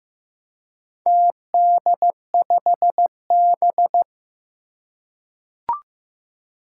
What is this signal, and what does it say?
Morse code, heard as a keyed tone, about 15 wpm, 700 Hz. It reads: TD5B
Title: Four Challenging Letter-Number Combinations 15wpm